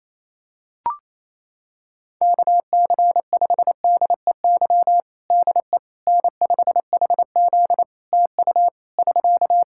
Morse code, heard as a keyed tone, about 28 wpm, 700 Hz. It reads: KC5DEY DE N5HZ TU <SK>
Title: Practice QSO - Exchange 4 - Ending - 28wpm